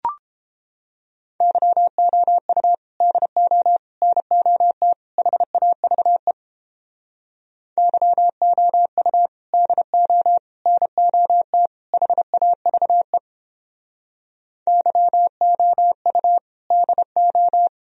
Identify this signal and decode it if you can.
Morse code, heard as a keyed tone, about 31 wpm, 700 Hz. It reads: YOU DO NOT HAVE YOU DO NOT HAVE YOU DO